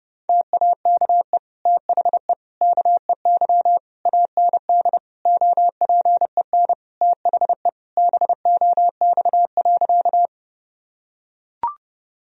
Morse code, heard as a keyed tone, about 30 wpm, 700 Hz. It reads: TAKE THE KEY AND OPEN THE BOX.
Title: Sentences from Top 1000 Words - Review Part 4 - 30wpm